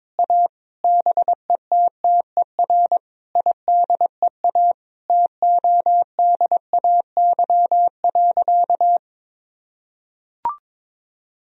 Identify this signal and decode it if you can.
Morse code, heard as a keyed tone, about 22 wpm, 700 Hz. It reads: A BETTER IDEA TODAY.
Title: Sentences from Top 1000 Words - Review Part 4 - 22wpm